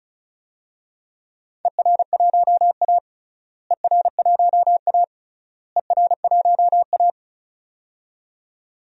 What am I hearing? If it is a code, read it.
Morse code, heard as a keyed tone, about 34 wpm, 700 Hz. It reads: ER1A ER1A ER1A
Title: DXCC-100 Call Signs - 4 Characters - Repeated 3x - 35wpm